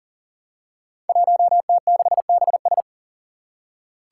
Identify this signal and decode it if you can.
Morse code, heard as a keyed tone, about 39 wpm, 700 Hz. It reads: 1T6BS
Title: Five Challenging Letter-Number Combinations 40wpm